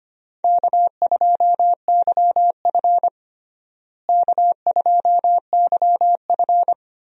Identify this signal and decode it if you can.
Morse code, heard as a keyed tone, about 25 wpm, 700 Hz. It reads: K2YF K2YF
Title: USA Call Signs - 4 Characters - Repeated 2x - 25wpm